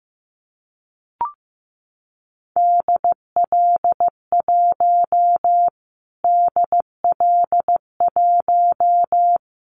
Morse code, heard as a keyed tone, about 15 wpm, 700 Hz. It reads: DL1 DL1
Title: Callsign Prefixes for Top 25 DXCC - Repeated 3x - 15wpm